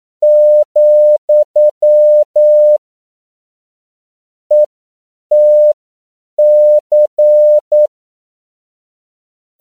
Morse code, heard as a keyed tone, about 9 wpm, spaced as for 7 wpm, 600 Hz. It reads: , ETC
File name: CW / 10PPM